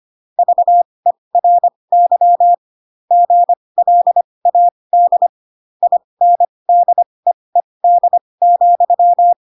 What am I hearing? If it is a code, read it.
Morse code, heard as a keyed tone, about 25 wpm, spaced as for 20 wpm, 700 Hz. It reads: VERY GLAD INDEED,
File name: Pinocchio / Pinocchio ch2 15 25 ik1hge